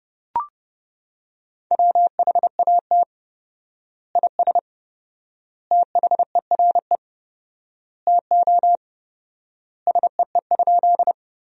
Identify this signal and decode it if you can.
Morse code, heard as a keyed tone, about 30 wpm, 700 Hz. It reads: WHAT IS THERE TO SEE?
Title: 4x Word-Spacing - Sentences from Top 100 Words - 30wpm